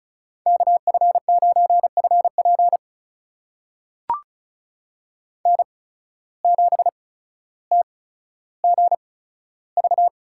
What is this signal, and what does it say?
Morse code, heard as a keyed tone, about 36 wpm, spaced as for 15 wpm, 700 Hz. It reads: KF9FP N7TGV
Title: USA Call Signs - 5 Characters - 8x Character Spacing - 35wpm